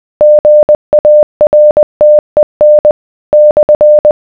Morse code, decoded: GARTEN /